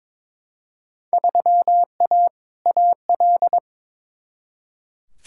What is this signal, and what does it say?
Morse code, heard as a keyed tone, about 21 wpm, 700 Hz. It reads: 3A AL
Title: Field Day Exchanges - 22wpm